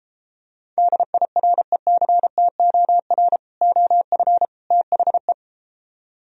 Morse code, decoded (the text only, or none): DIRECTOR OF THE